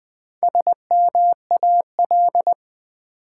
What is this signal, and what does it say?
Morse code, heard as a keyed tone, about 20 wpm, 700 Hz. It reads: SMAL
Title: Build A Word - 1 Letter At A Time - 5-Letter Words - 20wpm